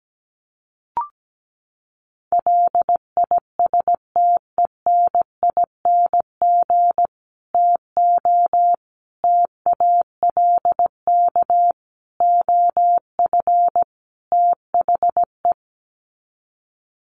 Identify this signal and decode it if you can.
Morse code, heard as a keyed tone, about 17 wpm, 700 Hz. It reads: LISTENING TO TALK OF THE
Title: Sets of 5 Words 17wpm